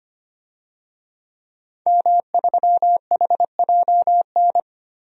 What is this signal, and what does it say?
Morse code, heard as a keyed tone, about 25 wpm, 700 Hz. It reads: M3HJN
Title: DXCC-100 Call Signs - 5 Characters - 25wpm